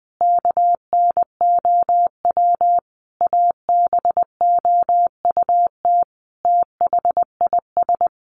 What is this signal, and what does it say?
Morse code, heard as a keyed tone, about 20 wpm, 700 Hz. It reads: KNOW ABOUT THIS
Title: Sentences from Top 600 Words 20wpm